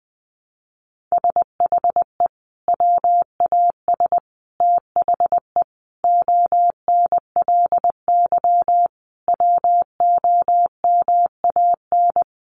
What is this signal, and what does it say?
Morse code, heard as a keyed tone, about 20 wpm, 700 Hz. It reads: SHE WAS THE ONLY WOMAN
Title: Sentences from Top 1100 Words 20wpm